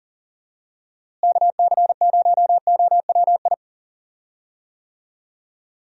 Morse code, heard as a keyed tone, about 40 wpm, 700 Hz. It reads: KC0OWI
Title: USA Call Signs - 6 Characters - Repeated 3x - 40wpm